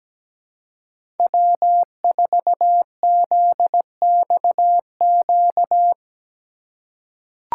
Morse code, heard as a keyed tone, about 17 wpm, 700 Hz. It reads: W4ZXQ
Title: USA Call Signs - 5 Characters - 2x Character Spacing - 17wpm